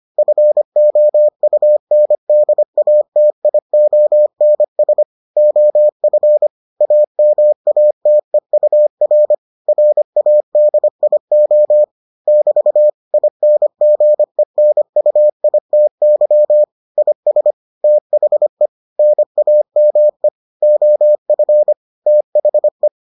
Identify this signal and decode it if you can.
Morse code, heard as a keyed tone, about 25 wpm, 600 Hz. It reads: FOUNDATIONS OF AMATEUR RADIO <BT> INGENUITY IS THE NAME OF THE